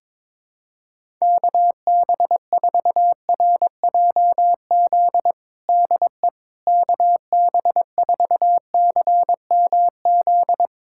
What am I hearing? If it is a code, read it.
Morse code, heard as a keyed tone, about 22 wpm, 700 Hz. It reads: KB4RJZ DE KB4CMZ
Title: USA Callsign DE Callsign - 6 Characters - 22wpm